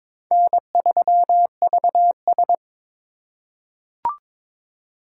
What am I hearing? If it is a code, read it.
Morse code, heard as a keyed tone, about 23 wpm, 700 Hz. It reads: N3VS
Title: USA Call Signs - 4 Characters - 4x Character Spacing - 22wpm